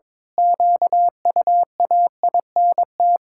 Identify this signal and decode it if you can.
Morse code, heard as a keyed tone, about 22 wpm, 700 Hz. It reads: QUAINT
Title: Pangram Challenge v2 - 22wpm